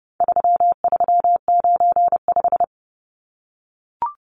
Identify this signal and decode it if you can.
Morse code, heard as a keyed tone, about 30 wpm, 700 Hz. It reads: 3395
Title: Numbers - 4-Digit 30wpm